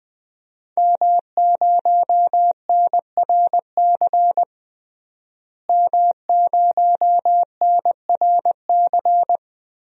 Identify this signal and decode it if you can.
Morse code, heard as a keyed tone, about 21 wpm, 700 Hz. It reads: M0NRC M0NRC
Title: DXCC-100 Call Signs - 5 Characters - Repeated 3x - 20wpm